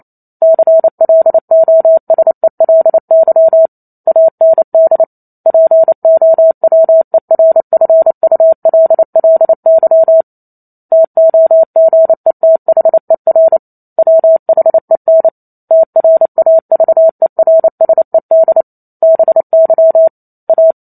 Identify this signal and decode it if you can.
Morse code, heard as a keyed tone, about 28 wpm, 650 Hz. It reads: CLOSELY AND POWERFULLY TOGETHER WHEN TRAVERSED BY A
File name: Chapter 2 - 28 WPM